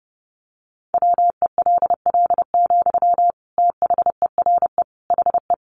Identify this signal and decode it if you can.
Morse code, heard as a keyed tone, about 30 wpm, 700 Hz. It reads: WELL, THERE HE
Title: Sentences from Top 500 Words 30wpm